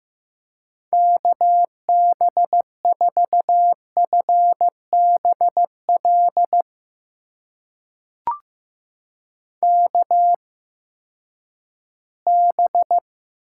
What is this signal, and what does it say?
Morse code, heard as a keyed tone, about 15 wpm, spaced as for 10 wpm, 700 Hz. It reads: KB4FBL KB
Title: USA Call Signs - 6 Characters - 8x Character Spacing - 15wpm